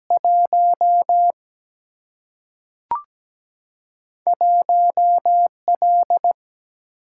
Morse code, heard as a keyed tone, about 17 wpm, 700 Hz. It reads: 1 1L
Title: Single-Letter Single-Number 17wpm